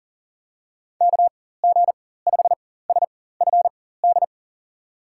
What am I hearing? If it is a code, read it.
Morse code, heard as a keyed tone, about 40 wpm, spaced as for 19 wpm, 700 Hz. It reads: KG5SFD
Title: USA Call Signs - 6 Characters - 4x Character Spacing - 40wpm